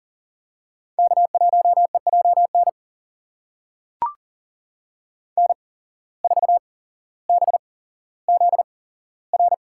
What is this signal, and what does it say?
Morse code, heard as a keyed tone, about 40 wpm, spaced as for 17 wpm, 700 Hz. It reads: K1EJN N4BZR
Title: USA Call Signs - 5 Characters - 8x Character Spacing - 40wpm